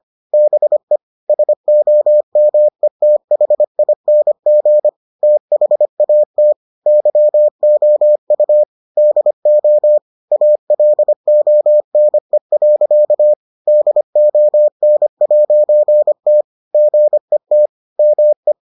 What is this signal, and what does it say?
Morse code, heard as a keyed tone, about 25 wpm, 600 Hz. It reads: BE SOMETHING THAT YOU DO ALONE. DON'T GET ME